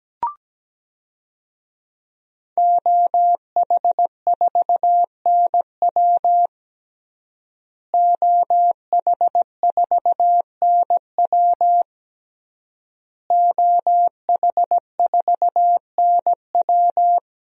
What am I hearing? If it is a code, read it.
Morse code, heard as a keyed tone, about 17 wpm, 700 Hz. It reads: OH4NW OH4NW OH4NW
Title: DXCC-100 Call Signs - 5 Characters - Repeated 3x - 17wpm